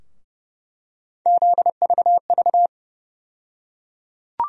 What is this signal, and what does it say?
Morse code, heard as a keyed tone, about 30 wpm, 700 Hz. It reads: ZVV
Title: USA Call Sign Postfixes - 3 Letters 30wpm